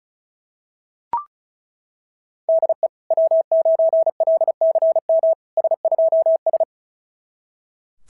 Morse code, heard as a keyed tone, about 36 wpm, 690 Hz. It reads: DE W9LCM S2S
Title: SOTA QSO Practice 35wpm